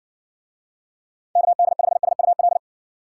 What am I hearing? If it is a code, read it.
Morse code, heard as a keyed tone, about 54 wpm, 700 Hz. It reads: KD5SUL